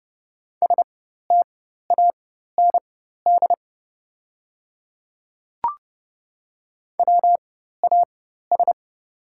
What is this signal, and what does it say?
Morse code, heard as a keyed tone, about 30 wpm, spaced as for 14 wpm, 700 Hz. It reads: STAND WAS
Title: 4x Character Spacing - Top 300 Words - 30wpm